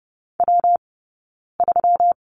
Morse code, heard as a keyed tone, about 30 wpm, 700 Hz. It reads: W 3
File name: Lesson 20d - Number 3 - Two Characters 30wpm